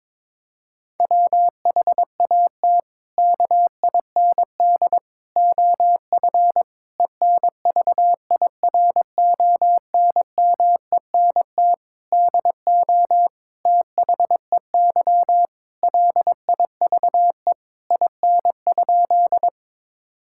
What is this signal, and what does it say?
Morse code, heard as a keyed tone, about 22 wpm, 700 Hz. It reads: WHAT KIND OF ENVIRONMENT DO THEY LIVE IN?
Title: Sentences from Top 1000 Words - Review Part 5 - 22wpm